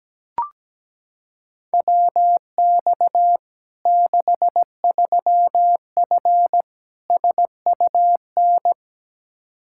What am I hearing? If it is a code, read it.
Morse code, heard as a keyed tone, about 17 wpm, 700 Hz. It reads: WX 63F SUN
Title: Practice QSO - Exchange 2 - Info - 17wpm